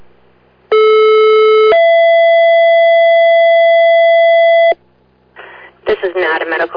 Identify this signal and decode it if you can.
Two-tone page, heard as a one-second tone, then a three-second tone, about 430 then 670 Hz, with speech. It is heard from 0.7 to 4.7 s.